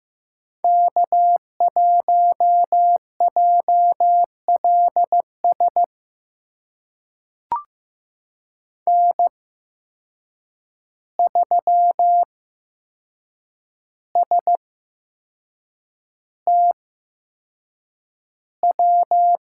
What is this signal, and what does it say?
Morse code, heard as a keyed tone, about 15 wpm, spaced as for 6 wpm, 700 Hz. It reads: K1JLS N3STW
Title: USA Call Signs - 5 Characters - 8x Character Spacing - 15wpm